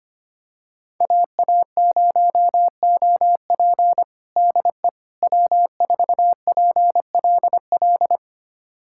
Morse code, heard as a keyed tone, about 25 wpm, 700 Hz. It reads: AA0OP DE W4PLL
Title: USA Callsign DE Callsign - 5 Characters - 25wpm